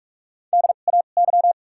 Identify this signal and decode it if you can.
Morse code, heard as a keyed tone, about 44 wpm, spaced as for 32 wpm, 700 Hz. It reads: DAY
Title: 2x Character Spacing - Top 200 Words - 45wpm